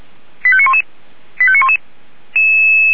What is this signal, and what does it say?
Five-tone selective call (ZVEI), heard as quick runs of five tones, tones of about 70 ms. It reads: 85911 85911
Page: /464/SDR/ZVEI/89542/